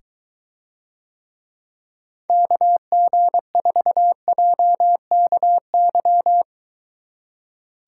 Morse code, heard as a keyed tone, about 23 wpm, 700 Hz. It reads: KG4JKY